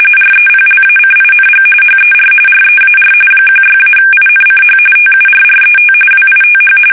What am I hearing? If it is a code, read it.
RTTY, about 45.45 baud. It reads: REPLACING HER REAR 16-INCH TURRET WITH
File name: fsk850 / battleshipnj